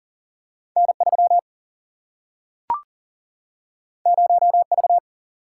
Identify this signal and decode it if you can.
Morse code, heard as a keyed tone, about 40 wpm, 700 Hz. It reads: N 3 0 V
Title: Single-Letter Single-Number 40wpm